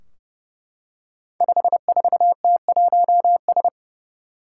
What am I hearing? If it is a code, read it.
Morse code, heard as a keyed tone, about 29 wpm, 700 Hz. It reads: 54T1S
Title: Five Challenging Letter-Number Combinations 30wpm